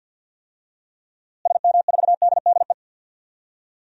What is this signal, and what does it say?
Morse code, heard as a keyed tone, about 46 wpm, 700 Hz. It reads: SM4DDE
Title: DXCC-100 Call Signs - 6 Characters - Repeated 2x - 50wpm